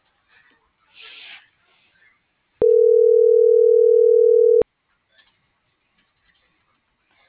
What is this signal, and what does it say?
Ringback tone (North America), 440 + 480 Hz.